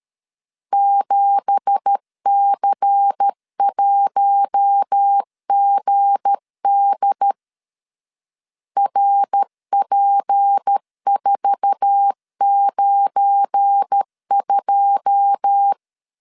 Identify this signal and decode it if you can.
Morse code, heard as a keyed tone, about 13 wpm, 800 Hz. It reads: 7C1GD RP492
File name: CW / DET / 1 9 1500C S A 12wpm 3800